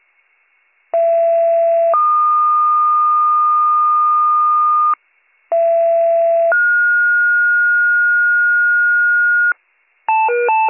Two-tone page, heard as a one-second tone, then a three-second tone, about 670 then 1220 Hz, about 670 then 1430 Hz, with noise. It is heard from 0.9 to 9.5 s.